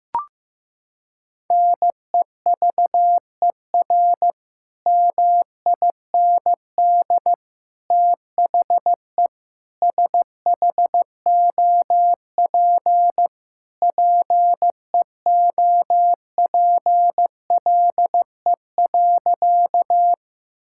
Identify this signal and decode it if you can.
Morse code, heard as a keyed tone, about 15 wpm, 700 Hz. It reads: NEVER MIND THE SHOP PEOPLE.
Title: Sentences from Top 1600 Words 15wpm